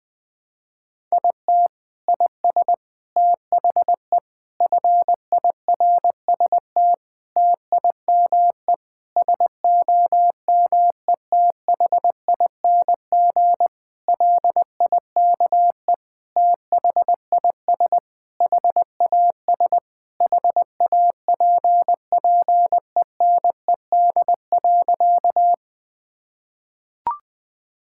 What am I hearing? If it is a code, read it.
Morse code, heard as a keyed tone, about 20 wpm, 700 Hz. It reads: IT IS THE FIRST TIME SOMETHING LIKE THIS HAS HAPPENED.
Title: Sentences with 8-Word Phrases - Spoken Beginning - 20wpm